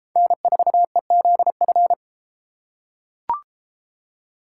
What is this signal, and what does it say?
Morse code, heard as a keyed tone, about 33 wpm, 700 Hz. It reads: N4EZF